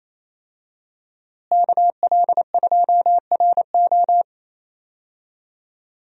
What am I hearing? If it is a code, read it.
Morse code, heard as a keyed tone, about 28 wpm, 700 Hz. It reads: KL2RO
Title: USA Call Signs - 5 Characters - Repeated 3x - 28wpm